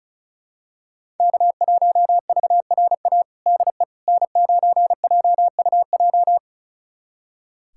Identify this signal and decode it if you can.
Morse code, heard as a keyed tone, about 34 wpm, 700 Hz. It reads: K1VRA DE N9JUJ
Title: USA Callsign DE Callsign - 5 Characters - 35wpm